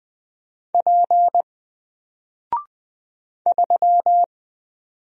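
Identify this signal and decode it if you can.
Morse code, heard as a keyed tone, about 20 wpm, 700 Hz. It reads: P 3
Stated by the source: Single letter and number Morse drill at 20 wpm, from N4LCD.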